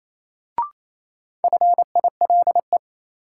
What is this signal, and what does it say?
Morse code, heard as a keyed tone, about 28 wpm, 700 Hz. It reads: FILE
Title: Sentences from Top 1000 Words - Review Part 4 - New Word Drill - 28wpm